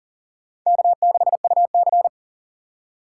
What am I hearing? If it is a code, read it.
Morse code, heard as a keyed tone, about 40 wpm, 700 Hz. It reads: K6UC
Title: USA Call Signs - 4 Characters - 4x Character Spacing - 40wpm